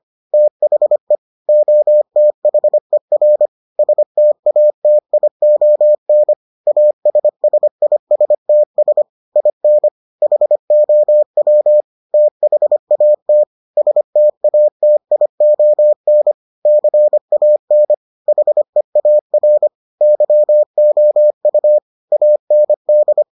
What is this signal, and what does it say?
Morse code, heard as a keyed tone, about 25 wpm, 600 Hz. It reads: THE OTHER STATION ASSISTS IN HOW THAT STATION CAN HEAR YOU AND